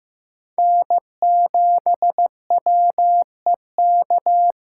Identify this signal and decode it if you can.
Morse code, heard as a keyed tone, about 18 wpm, 700 Hz. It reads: N7WEK
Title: Speed Racing - USA 1x3 Call Signs 15wpm